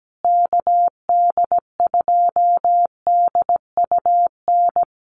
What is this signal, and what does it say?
Morse code, heard as a keyed tone, about 17 wpm, 700 Hz. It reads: KD2DUN
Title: USA Call Signs Repeated 2x 17wpm